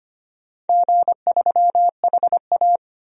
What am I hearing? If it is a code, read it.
Morse code, heard as a keyed tone, about 25 wpm, 700 Hz. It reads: G3HA
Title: DXCC-100 Call Signs - 4 Characters - 25wpm